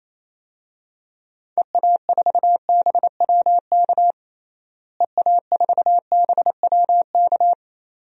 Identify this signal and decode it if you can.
Morse code, heard as a keyed tone, about 28 wpm, 700 Hz. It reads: EA4BWK EA4BWK
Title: DXCC-100 Call Signs - 6 Characters - Repeated 2x - 28wpm